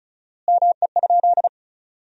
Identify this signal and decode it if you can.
Morse code, heard as a keyed tone, about 36 wpm, 700 Hz. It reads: ME?
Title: 4x Word-Spacing - Sentences from Top 300 Words - 35wpm